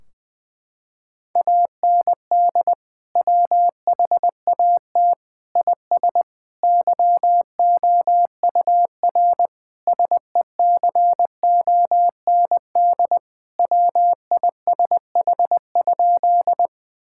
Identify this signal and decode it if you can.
Morse code, heard as a keyed tone, about 20 wpm, 700 Hz. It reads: AND WHAT IS YOUR SECOND WISH?
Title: Sentences from Top 1600 Words 20wpm